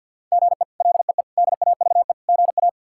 Morse code, heard as a keyed tone, about 50 wpm, 700 Hz. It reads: GE LEE DAVE GA
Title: K1USN SST QSO Practice - 50wpm